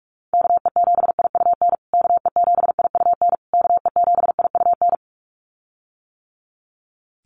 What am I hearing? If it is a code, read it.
Morse code, heard as a keyed tone, about 44 wpm, 700 Hz. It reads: KE7IUN KE7IUN KE7IUN
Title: USA Call Signs Repeated 3x 45wpm